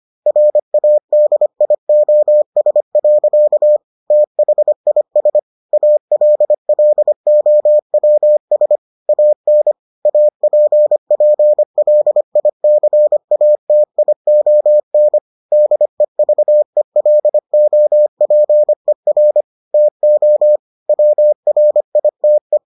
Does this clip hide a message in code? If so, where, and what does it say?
Yes: Morse code from 0.3 to 22.6 s, heard as a keyed tone, about 25 wpm, 600 Hz. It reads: RADIOS. THIS ALLOWS AN APPLICATION DEVELOPER TO WRITE